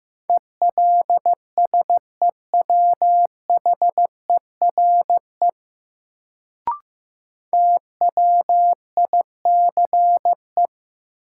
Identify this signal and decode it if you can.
Morse code, heard as a keyed tone, about 15 wpm, 700 Hz. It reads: ELSEWHERE TWICE
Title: Sentences from Top 1600 Words - New Word Drill - 15wpm